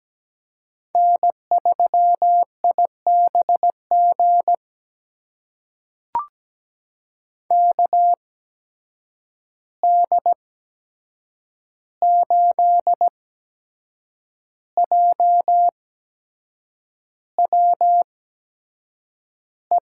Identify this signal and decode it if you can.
Morse code, heard as a keyed tone, about 17 wpm, spaced as for 7 wpm, 700 Hz. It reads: N3IBG KD8JWE
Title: USA Call Signs - 8x Character Spacing - 17wpm